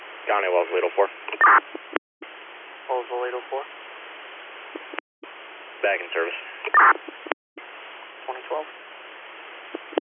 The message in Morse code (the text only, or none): none